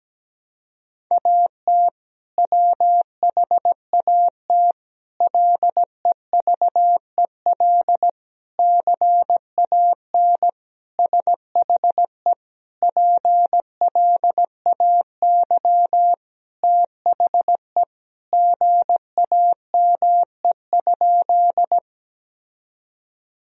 Morse code, heard as a keyed tone, about 17 wpm, 700 Hz. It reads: AT WHAT LEVEL CAN SHE PLAY THE GAME?
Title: Sentences from Top 1000 Words - Review Part 4 - 17wpm